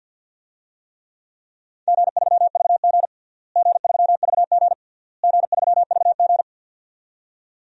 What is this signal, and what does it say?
Morse code, heard as a keyed tone, about 52 wpm, 700 Hz. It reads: G3VG G3VG G3VG
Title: DXCC-100 Call Signs - 4 Characters - Repeated 3x - 50wpm